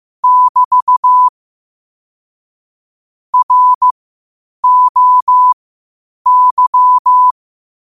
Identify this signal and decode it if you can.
Morse code, heard as a keyed tone, about 15 wpm, spaced as for 8 wpm, 1000 Hz. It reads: <BT> ROY
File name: qotdfiles / 8wpm